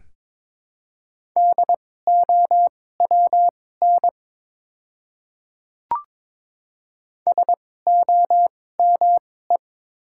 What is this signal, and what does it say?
Morse code, heard as a keyed tone, about 22 wpm, spaced as for 16 wpm, 700 Hz. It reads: DOWN SOME
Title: 2x Character Spacing - Top 100 Words - 22wpm